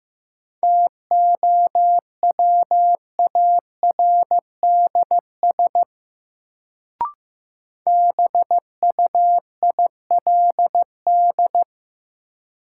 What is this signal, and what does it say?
Morse code, heard as a keyed tone, about 15 wpm, 700 Hz. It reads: TOWARDS BUILD
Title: Sentences from Top 1100 Words - New Word Drill Alt - 15wpm